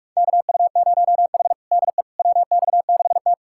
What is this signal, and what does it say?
Morse code, heard as a keyed tone, about 46 wpm, 700 Hz. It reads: KU0H DE WX6T
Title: USA Callsign DE Callsign - 4 Characters - 45wpm